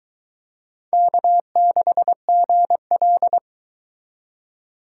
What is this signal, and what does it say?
Morse code, heard as a keyed tone, about 21 wpm, 700 Hz. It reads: K6GL